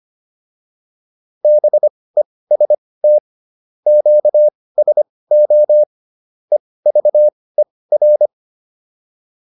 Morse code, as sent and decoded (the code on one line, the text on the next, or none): -... ... - / --.- ... --- / . ...- . .-.
BEST QSO EVER